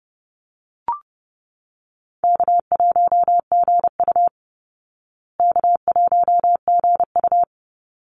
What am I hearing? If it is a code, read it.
Morse code, heard as a keyed tone, about 30 wpm, 700 Hz. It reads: K1GU K1GU
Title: Easy Call Signs Repeated 2x - 30wpm